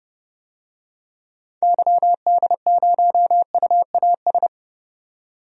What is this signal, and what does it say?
Morse code, heard as a keyed tone, about 31 wpm, 700 Hz. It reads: YD0UAS
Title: DXCC-100 Call Signs - 6 Characters - 30wpm